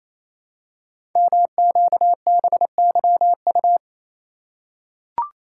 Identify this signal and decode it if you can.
Morse code, heard as a keyed tone, about 27 wpm, 700 Hz. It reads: MQBYU
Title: Code Groups - 5 Letters - 28wpm